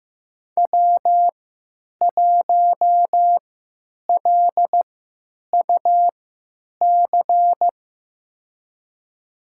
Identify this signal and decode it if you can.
Morse code, heard as a keyed tone, about 15 wpm, spaced as for 9 wpm, 700 Hz. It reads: W1LUC
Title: USA Call Signs - 3x Character Spacing - 15wpm